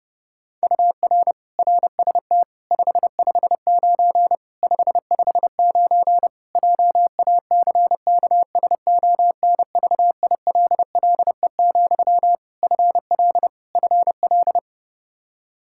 Morse code, heard as a keyed tone, about 30 wpm, 700 Hz. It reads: UR RST 559 559 JACKSONVILLE, FL FL
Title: Practice QSO - Exchange 1 - Beginning - 30wpm